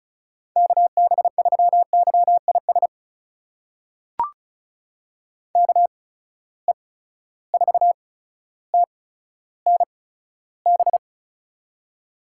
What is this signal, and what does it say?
Morse code, heard as a keyed tone, about 34 wpm, spaced as for 15 wpm, 700 Hz. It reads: KB3YIS KE4TNB
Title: USA Call Signs - 6 Characters - 8x Character Spacing - 35wpm